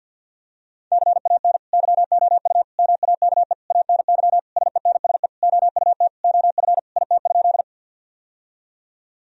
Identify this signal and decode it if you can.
Morse code, heard as a keyed tone, about 50 wpm, 700 Hz. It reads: CAN YOU MAKE ANY SENSE OUT OF IT?
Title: Sentences from Top 1000 Words - Review Part 4 - 50wpm